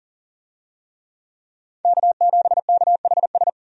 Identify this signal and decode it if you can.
Morse code, heard as a keyed tone, about 40 wpm, 700 Hz. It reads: K7KHS
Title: USA Call Signs - 5 Characters - 40wpm